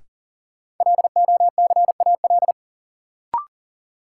Morse code, heard as a keyed tone, about 40 wpm, 700 Hz. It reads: LOCAL
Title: Sentences from Top 1500 Words - New Word Drill Alt - 40wpm